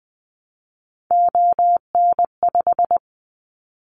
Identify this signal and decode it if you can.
Morse code, heard as a keyed tone, about 20 wpm, 700 Hz. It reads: ON5
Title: Callsign Prefixes for Top 25 DXCC - 20wpm